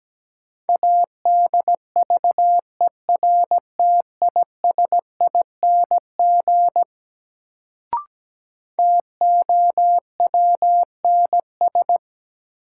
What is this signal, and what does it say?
Morse code, heard as a keyed tone, about 17 wpm, 700 Hz. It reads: ADVERTISING TOWNS
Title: Sentences from Top 1000 Words - Review Part 6 - New Word Drill - 17wpm